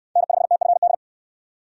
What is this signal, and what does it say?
Morse code, heard as a keyed tone, about 62 wpm, 700 Hz. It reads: N5TFD